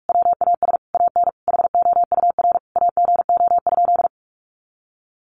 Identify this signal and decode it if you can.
Morse code, heard as a keyed tone, about 44 wpm, 700 Hz. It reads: WAS AN HOUR AGO?
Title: Sentences from Top 1300 Words 45wpm